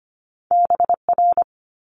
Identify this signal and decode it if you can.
Morse code, heard as a keyed tone, about 25 wpm, 700 Hz. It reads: BR
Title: Multi-letter Phonemes Quick Fire 25wpm